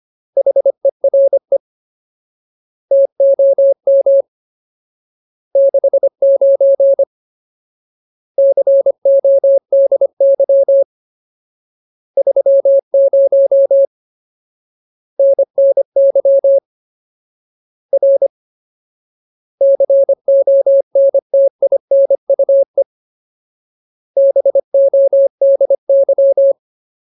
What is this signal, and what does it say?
Morse code, heard as a keyed tone, about 25 wpm, 550 Hz. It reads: HERE TOM 69 CODY 30 NNY R CONTINUE BODY
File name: Ultimate Word List / 25WPM Ultimate